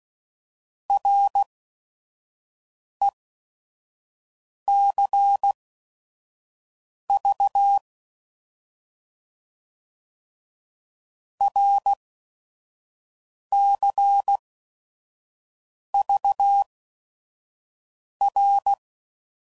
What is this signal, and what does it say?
Morse code, heard as a keyed tone, about 16 wpm, spaced as for 5 wpm, 800 Hz. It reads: RECV RCVR
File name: Mixed / Abreviations / Abreviations